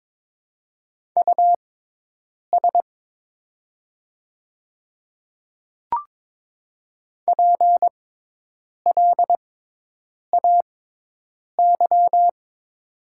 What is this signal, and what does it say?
Morse code, heard as a keyed tone, about 22 wpm, spaced as for 8 wpm, 700 Hz. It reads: US PLAY
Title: 6x Character Spacing - Top 200 Words - 22wpm